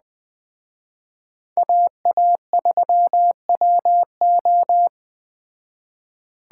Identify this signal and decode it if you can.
Morse code, heard as a keyed tone, about 20 wpm, 700 Hz. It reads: AA3WO
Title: Speed Racing - USA 2x2 Call Signs 17wpm